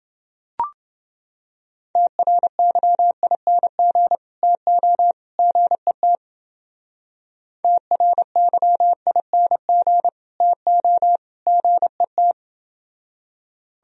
Morse code, heard as a keyed tone, about 29 wpm, 700 Hz. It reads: TRYING TO GET TRYING TO GET